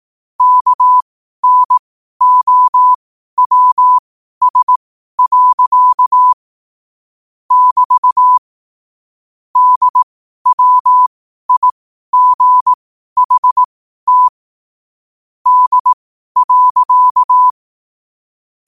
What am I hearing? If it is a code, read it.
Morse code, heard as a keyed tone, about 18 wpm, spaced as for 12 wpm, 1000 Hz. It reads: KNOWS. <BT> DWIGHT D.
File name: qotdfiles / 12wpm